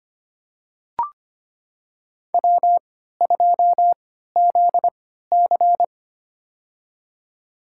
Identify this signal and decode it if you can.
Morse code, heard as a keyed tone, about 25 wpm, spaced as for 14 wpm, 700 Hz. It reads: W2ZC
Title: USA Call Signs - 4 Characters - 3x Character Spacing - 25wpm